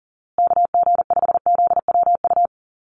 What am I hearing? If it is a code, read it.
Morse code, heard as a keyed tone, about 40 wpm, 700 Hz. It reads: KG5ZWU
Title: USA Call Signs Repeated 3x 40wpm